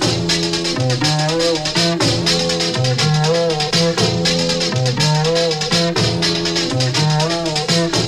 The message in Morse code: none